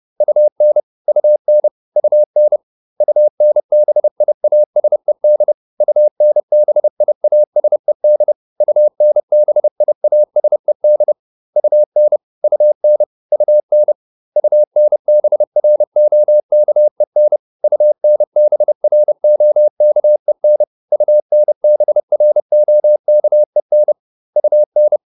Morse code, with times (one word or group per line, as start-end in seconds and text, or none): 0.2-0.8 UN
1.1-1.7 UN
2.0-2.6 UN
3.0-5.5 UNBIASED
5.8-8.3 UNBIASED
8.6-11.1 UNBIASED
11.6-12.2 UN
12.4-13.0 UN
13.3-13.9 UN
14.4-17.4 UNBROKEN
17.6-20.6 UNBROKEN
20.9-23.9 UNBROKEN
24.4-25.0 UN